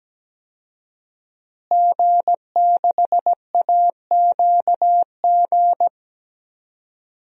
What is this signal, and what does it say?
Morse code, heard as a keyed tone, about 17 wpm, 700 Hz. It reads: G6AQG
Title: DXCC-100 Call Signs - 5 Characters - Repeated 3x - 17wpm